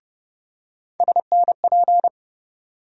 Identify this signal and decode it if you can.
Morse code, heard as a keyed tone, about 30 wpm, 700 Hz. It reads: SNP
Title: USA Call Sign Postfixes - 3 Letters 30wpm